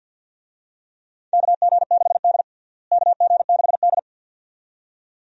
Morse code, heard as a keyed tone, about 53 wpm, 700 Hz. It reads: KG6D KG6D
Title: USA Call Signs - 4 Characters - Repeated 2x - 50wpm